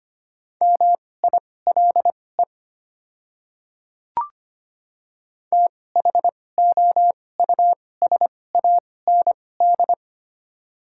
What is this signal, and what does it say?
Morse code, heard as a keyed tone, about 25 wpm, spaced as for 18 wpm, 700 Hz. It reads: MILE THOUSAND
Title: 2x Character Spacing - Top 500 Words - 25wpm